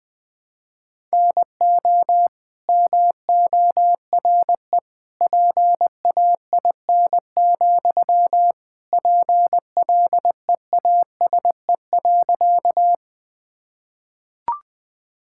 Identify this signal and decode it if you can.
Morse code, heard as a keyed tone, about 20 wpm, 700 Hz. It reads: NO MORE PAIN, PLEASE.
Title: Sentences from Top 1600 Words 20wpm